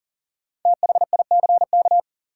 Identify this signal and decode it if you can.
Morse code, heard as a keyed tone, about 40 wpm, 700 Hz. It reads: THICK
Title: Sentences from Top 1600 Words - New Word Drill - 40wpm